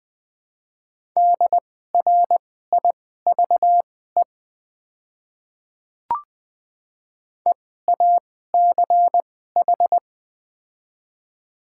Morse code, heard as a keyed tone, about 20 wpm, spaced as for 14 wpm, 690 Hz. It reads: DRIVE EACH
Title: 2x Character Spacing - Top 500 Words - 20wpm